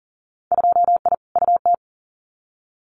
Morse code, heard as a keyed tone, about 41 wpm, 700 Hz. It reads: 2I UT
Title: Winter Field Day Exchanges 40wpm